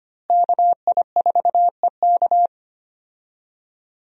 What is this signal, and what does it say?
Morse code, heard as a keyed tone, about 25 wpm, 700 Hz. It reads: KI4EK
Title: USA Call Signs - 5 Characters - Repeated 3x - 25wpm